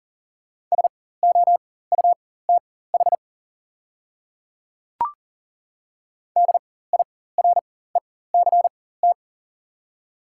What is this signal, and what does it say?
Morse code, heard as a keyed tone, about 37 wpm, spaced as for 18 wpm, 690 Hz. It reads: SOUTH DIRECT
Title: 4x Character Spacing - Top 500 Words - 40wpm